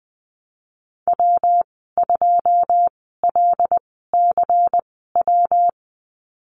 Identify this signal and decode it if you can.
Morse code, heard as a keyed tone, about 20 wpm, spaced as for 15 wpm, 700 Hz. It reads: W2LCW
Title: 179 Lesson 29g - Slash - Call Signs 20wpm